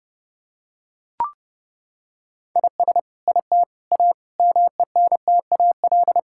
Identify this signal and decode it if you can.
Morse code, heard as a keyed tone, about 28 wpm, 700 Hz. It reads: IS IT A MENTAL
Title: Sentences from Top 1000 Words - Review Part 6 - 30wpm